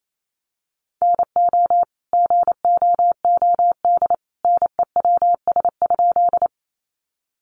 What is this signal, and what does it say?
Morse code, heard as a keyed tone, about 28 wpm, 700 Hz. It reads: NO GOOD NEWS?
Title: Sentences from Top 800 Words 28wpm